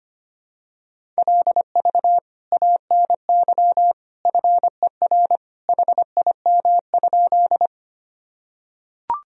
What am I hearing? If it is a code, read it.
Morse code, heard as a keyed tone, about 25 wpm, 700 Hz. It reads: LV ANY FER HIM?
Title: Sentences from Top 400 Words and Abbreviations 25wpm